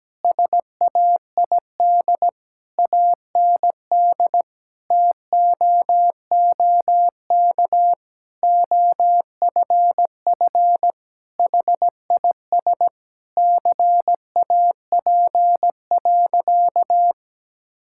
Morse code, heard as a keyed tone, about 17 wpm, 700 Hz. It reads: SAID AND TOOK OFF HIS CAP.
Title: Sentences from Top 1600 Words 17wpm